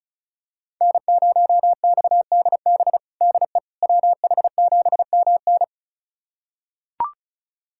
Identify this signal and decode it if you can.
Morse code, heard as a keyed tone, about 36 wpm, 700 Hz. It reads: N0XDB DE WH7MN
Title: USA Callsign DE Callsign - 5 Characters - 35wpm